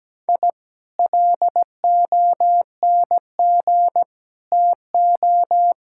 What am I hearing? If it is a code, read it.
Morse code, heard as a keyed tone, about 17 wpm, 700 Hz. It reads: I LONG TO
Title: Sentences from Top 1000 Words - Review Part 6 - 17wpm